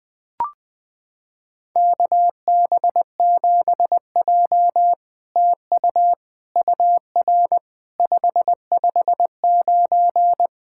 Morse code, heard as a keyed tone, about 20 wpm, 700 Hz. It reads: KB7J TU UR 559
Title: SOTA QSO Practice 20wpm